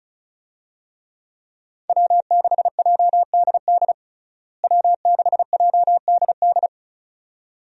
Morse code, heard as a keyed tone, about 36 wpm, 700 Hz. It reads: W6JDD W6JDD
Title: USA Call Signs - 5 Characters - Repeated 2x - 35wpm